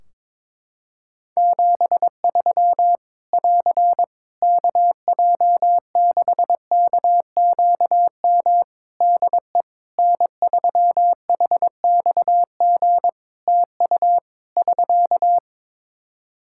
Morse code, heard as a keyed tone, about 22 wpm, 700 Hz. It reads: 73 <AR> KJ6KQM DE N3HXG TU <SK>
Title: Practice QSO - Exchange 4 - Ending - 22wpm